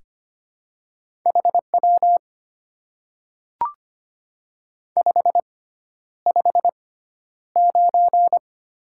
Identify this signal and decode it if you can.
Morse code, heard as a keyed tone, about 25 wpm, spaced as for 11 wpm, 700 Hz. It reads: HW 559
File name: Top 90 QSO Elements - 6x Character Spacing - 25wpm